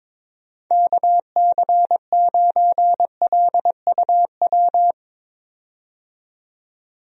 Morse code, heard as a keyed tone, about 22 wpm, 700 Hz. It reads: KC9LUW
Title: USA Call Signs - 6 Characters - 22wpm